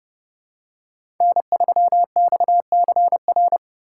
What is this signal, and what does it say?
Morse code, heard as a keyed tone, about 30 wpm, 700 Hz. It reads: N3XCR
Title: USA Call Signs - 5 Characters - 4x Character Spacing - 30wpm